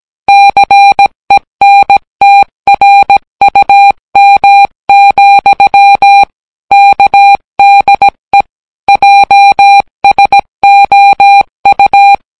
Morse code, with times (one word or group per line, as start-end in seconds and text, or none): 0.3-6.2 CENTRUM,
6.7-8.4 KDE
8.9-12.2 JSOU